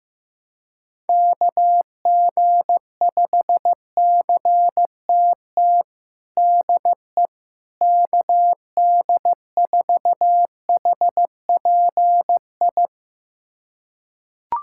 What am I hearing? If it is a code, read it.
Morse code, heard as a keyed tone, about 15 wpm, 700 Hz. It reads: KG5CTT DE KD4HPI
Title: USA Callsign DE Callsign 15wpm